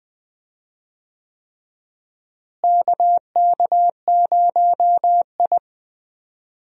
Morse code, heard as a keyed tone, about 20 wpm, 700 Hz. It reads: KK0I
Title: USA Call Signs - 4 Characters - Repeated 3x - 20wpm